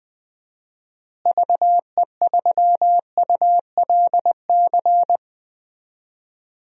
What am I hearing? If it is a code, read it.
Morse code, heard as a keyed tone, about 21 wpm, 700 Hz. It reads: VE3ULC
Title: DXCC-100 Call Signs - 6 Characters - Repeated 2x - 20wpm